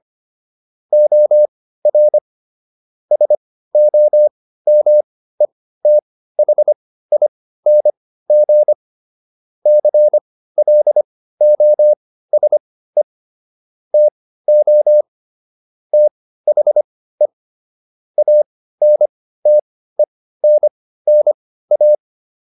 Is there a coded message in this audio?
Yes: Morse code from 0.9 to 21.9 s, heard as a keyed tone, about 25 wpm, spaced as for 15 wpm, 600 Hz. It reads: OR SOMETHING CLOSE TO THE ANTENNA